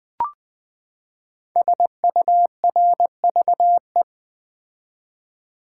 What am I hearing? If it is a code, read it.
Morse code, heard as a keyed tone, about 20 wpm, 700 Hz. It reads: SURVE
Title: Build A Word - 5 Letters At A Time - 12-Letter Words - 20wpm